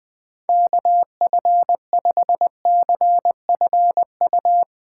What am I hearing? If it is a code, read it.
Morse code, heard as a keyed tone, about 20 wpm, 700 Hz. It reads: KF5CFU
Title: USA Call Signs - 6 Characters - 4x Character Spacing - 20wpm